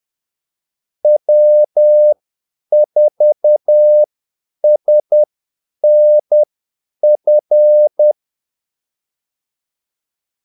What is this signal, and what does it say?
Morse code, heard as a keyed tone, about 10 wpm, spaced as for 8 wpm, 600 Hz. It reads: W4SNF
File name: Callsigns-10-8wpm